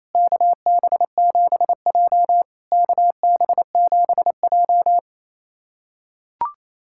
Morse code, heard as a keyed tone, about 28 wpm, 700 Hz. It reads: KB7J KB7J
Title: SOTA QSO Practice 28wpm